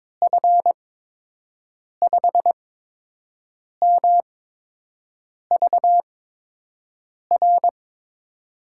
Morse code, heard as a keyed tone, about 22 wpm, spaced as for 6 wpm, 700 Hz. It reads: F5MVR
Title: USA Call Signs - 6 Characters - 8x Character Spacing - 22wpm